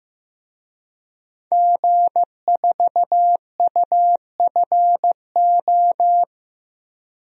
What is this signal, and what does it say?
Morse code, heard as a keyed tone, about 16 wpm, 700 Hz. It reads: G4UFO
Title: DXCC-100 Call Signs - 5 Characters - Repeated 2x - 15wpm